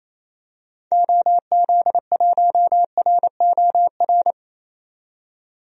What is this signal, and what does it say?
Morse code, heard as a keyed tone, about 28 wpm, 700 Hz. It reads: OZ1ROR
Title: DXCC-100 Call Signs - 6 Characters - 28wpm